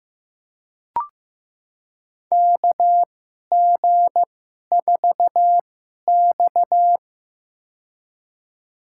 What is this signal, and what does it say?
Morse code, heard as a keyed tone, about 16 wpm, spaced as for 11 wpm, 700 Hz. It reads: KG4X
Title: USA Call Signs - 2x Character Spacing - 15wpm